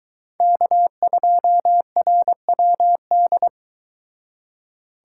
Morse code, heard as a keyed tone, about 22 wpm, 700 Hz. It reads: K2RWD